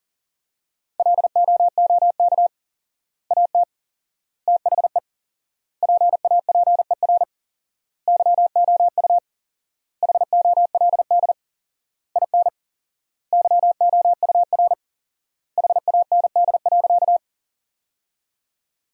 Morse code, heard as a keyed tone, about 40 wpm, 700 Hz. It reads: LOOK AT THE PAPER YOU HOLD IN YOUR HAND.
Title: 4x Word-Spacing - Sentences from Top 400 Words - 40wpm